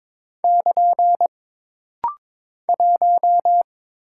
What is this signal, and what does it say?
Morse code, heard as a keyed tone, about 22 wpm, 700 Hz. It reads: <KN> 1
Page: This rapid-fire timing is geared to push your ICR (Instant Character Recognition) skills.